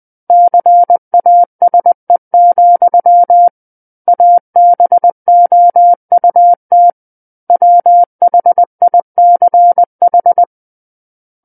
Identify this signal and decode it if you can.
Morse code, heard as a keyed tone, about 20 wpm, 700 Hz. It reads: CASE, ABOUT WHICH
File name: 20 WPM